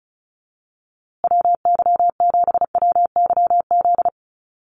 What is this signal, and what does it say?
Morse code, heard as a keyed tone, about 34 wpm, 700 Hz. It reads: WY7WYZ
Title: USA Call Signs Repeated 2x 35wpm